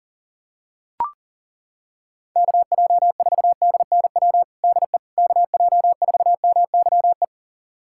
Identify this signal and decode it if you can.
Morse code, heard as a keyed tone, about 40 wpm, 700 Hz. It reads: KJ4DNW DE KJ4MYE
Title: USA Callsign DE Callsign - 6 Characters - 40wpm